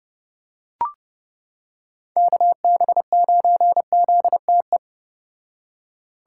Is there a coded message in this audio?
Yes: Morse code from 2.2 to 4.8 s, heard as a keyed tone, about 30 wpm, 700 Hz. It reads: KB9ZTE